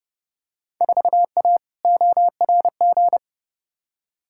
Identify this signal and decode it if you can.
Morse code, heard as a keyed tone, about 32 wpm, 700 Hz. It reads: 4A ORG
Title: Field Day Exchanges - 30wpm